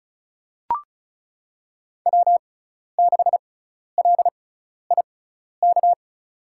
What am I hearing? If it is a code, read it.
Morse code, heard as a keyed tone, about 36 wpm, spaced as for 12 wpm, 700 Hz. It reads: W6LIK
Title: USA Call Signs - 5 Characters - 6x Character Spacing - 35wpm